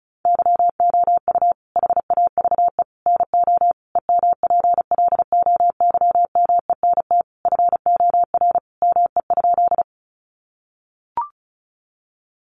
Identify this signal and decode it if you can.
Morse code, heard as a keyed tone, about 34 wpm, 700 Hz. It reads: YOU HAVE NO EMPLOYMENT FOR ME?
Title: Sentences from Top 1500 Words 35wpm